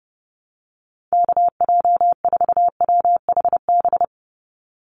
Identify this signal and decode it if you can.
Morse code, heard as a keyed tone, about 30 wpm, 700 Hz. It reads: KJ4WHB
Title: USA Call Signs 30wpm